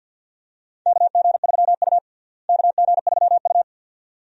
Morse code, heard as a keyed tone, about 51 wpm, 700 Hz. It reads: KG3U KG3U
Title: USA Call Signs - 4 Characters - Repeated 2x - 50wpm